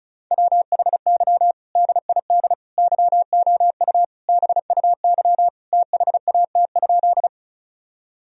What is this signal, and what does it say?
Morse code, heard as a keyed tone, about 36 wpm, 700 Hz. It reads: WHY DID YOU BUY THAT?
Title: Sentences from Top 1000 Words - Review Part 4 - 35wpm